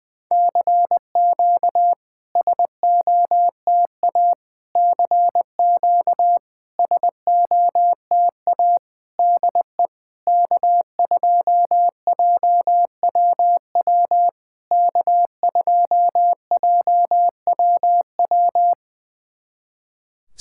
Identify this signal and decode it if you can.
Morse code, heard as a keyed tone, about 21 wpm, 700 Hz. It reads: CQ SOTA CQ SOTA DE K2JWW K2JWW
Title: SOTA QSO Practice 20wpm